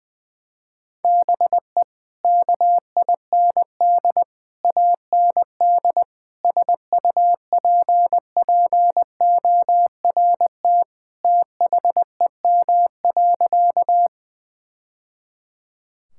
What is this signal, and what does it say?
Morse code, heard as a keyed tone, about 20 wpm, 700 Hz. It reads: BE KIND AND SUPPORT THEM.
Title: Sentences from Top 1000 Words - Review Part 5 - 20wpm